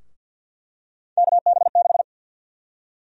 Morse code, heard as a keyed tone, about 47 wpm, 700 Hz. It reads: KB6
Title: USA Call Sign Prefixes - 50wpm